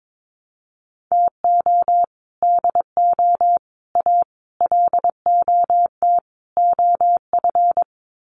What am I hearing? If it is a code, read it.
Morse code, heard as a keyed tone, about 22 wpm, 700 Hz. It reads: TO DO A LOT OF
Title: Sets of 5 Words Encore 22wpm